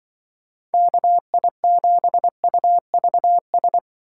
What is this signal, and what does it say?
Morse code, heard as a keyed tone, about 24 wpm, 700 Hz. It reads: KI7UVS
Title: Speed Racing - USA 2x3 Call Signs 20wpm